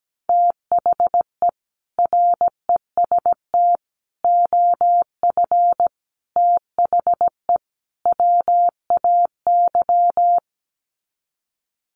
Morse code, heard as a keyed tone, about 17 wpm, 700 Hz. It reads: THE REST OF THE WAY
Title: Sets of 5 Words Encore 17wpm